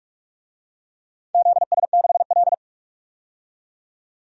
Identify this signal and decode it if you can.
Morse code, heard as a keyed tone, about 46 wpm, 700 Hz. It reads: ZS6L
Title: DXCC-100 Call Signs - 4 Characters - 45wpm